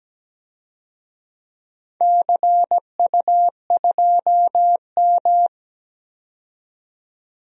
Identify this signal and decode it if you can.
Morse code, heard as a keyed tone, about 17 wpm, 700 Hz. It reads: CU2M
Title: DXCC-100 Call Signs - 4 Characters - 17wpm